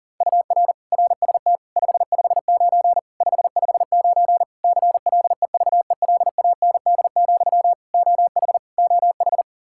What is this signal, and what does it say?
Morse code, heard as a keyed tone, about 40 wpm, 700 Hz. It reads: UR RST 559 559 CLEVELAND, OH OH
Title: K1USN SST QSO Practice 40wpm